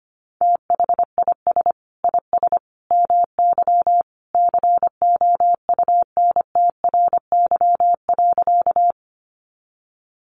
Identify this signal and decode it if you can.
Morse code, heard as a keyed tone, about 25 wpm, 700 Hz. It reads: THIS IS MY COUNTRY.
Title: Sentences from Top 300 Words 25wpm